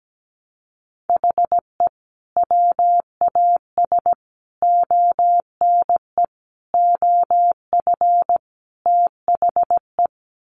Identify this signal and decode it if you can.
Morse code, heard as a keyed tone, about 17 wpm, 700 Hz. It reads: HE WAS ONE OF THE
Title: Sets of 5 Words 17wpm